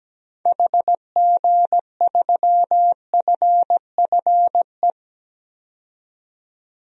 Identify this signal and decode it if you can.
Morse code, heard as a keyed tone, about 17 wpm, 700 Hz. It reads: HG3FFE
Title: DXCC-100 Call Signs - 6 Characters - 17wpm